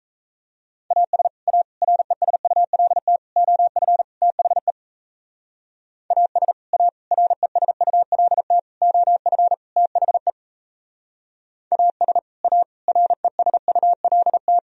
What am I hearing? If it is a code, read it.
Morse code, heard as a keyed tone, about 37 wpm, 700 Hz. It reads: AS A RESULT OF THE AS A RESULT OF THE AS A RESULT